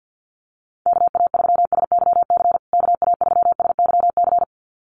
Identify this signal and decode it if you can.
Morse code, heard as a keyed tone, about 50 wpm, 700 Hz. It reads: KA3SYC KA3SYC
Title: USA Call Signs Repeated 2x 50wpm